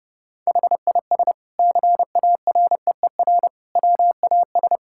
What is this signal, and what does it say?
Morse code, heard as a keyed tone, about 32 wpm, 700 Hz. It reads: HIS CAREER WAS
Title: Sentences from Top 1000 Words - Review Part 6 - 30wpm